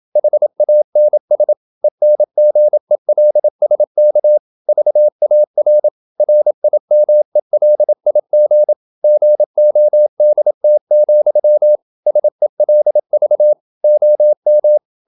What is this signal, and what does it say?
Morse code, heard as a keyed tone, about 27 wpm, 600 Hz. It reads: HANS ENGELSK VAR RIMELIG GODT, SELV OM
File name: Ferie QRQ 27wpm